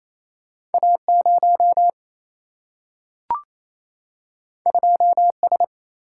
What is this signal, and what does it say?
Morse code, heard as a keyed tone, about 28 wpm, 700 Hz. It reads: A0 2S
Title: Single-Letter Single-Number 28wpm